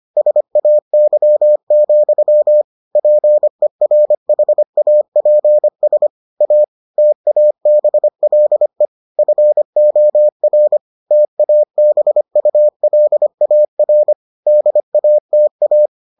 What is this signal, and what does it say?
Morse code, heard as a keyed tone, about 25 wpm, 600 Hz. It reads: SAY, PERHAPS A TABLE FOR TABULAR DATA